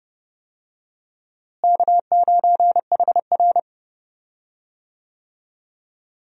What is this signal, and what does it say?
Morse code, heard as a keyed tone, about 30 wpm, 700 Hz. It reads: K9HR
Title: USA Call Signs - 4 Characters - Repeated 3x - 30wpm